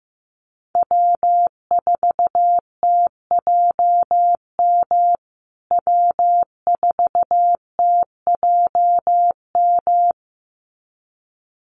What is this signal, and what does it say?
Morse code, heard as a keyed tone, about 15 wpm, 700 Hz. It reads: W4TJM W4TJM
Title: USA Call Signs Repeated 2x 15wpm